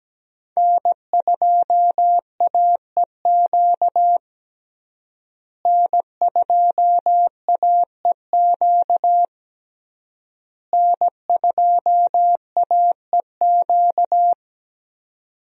Morse code, heard as a keyed tone, about 17 wpm, 700 Hz. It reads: N2AEQ N2AEQ N2AEQ
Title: USA Call Signs - 5 Characters - Repeated 3x - 17wpm